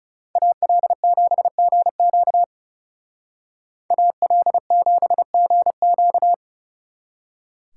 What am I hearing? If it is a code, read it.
Morse code, heard as a keyed tone, about 38 wpm, 700 Hz. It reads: AL7GQ AL7GQ